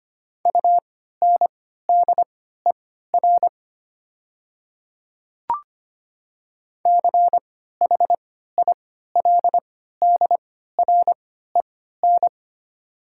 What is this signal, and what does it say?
Morse code, heard as a keyed tone, about 25 wpm, spaced as for 14 wpm, 690 Hz. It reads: UNDER CHILDREN
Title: 3x Character Spacing - Top 300 Words - 25wpm